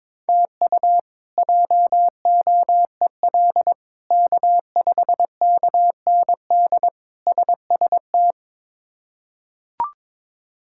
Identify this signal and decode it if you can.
Morse code, heard as a keyed tone, about 22 wpm, 700 Hz. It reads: TU JOEL K5KND SST
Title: K1USN SST QSO Practice - 22wpm